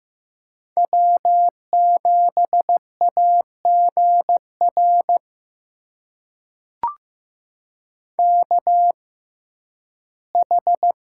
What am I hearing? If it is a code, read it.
Morse code, heard as a keyed tone, about 15 wpm, spaced as for 11 wpm, 700 Hz. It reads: W7AGR KH
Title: USA Call Signs - 5 Characters - 6x Character Spacing - 15wpm